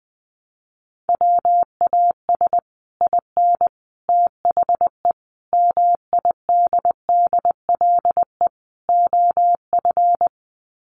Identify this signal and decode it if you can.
Morse code, heard as a keyed tone, about 20 wpm, 700 Hz. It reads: WAS IN THE MIDDLE OF
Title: Sets of 5 Words Encore 20wpm